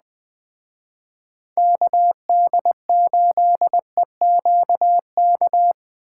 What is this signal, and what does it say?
Morse code, heard as a keyed tone, about 20 wpm, 700 Hz. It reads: KD8EQK
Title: Speed Racing - USA 2x3 Call Signs 17wpm